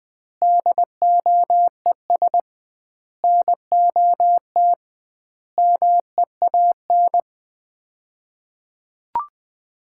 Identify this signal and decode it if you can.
Morse code, heard as a keyed tone, about 20 wpm, 700 Hz. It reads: DOES NOT MEAN
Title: 2x Word-Spacing - Sets of 5 Words - 20wpm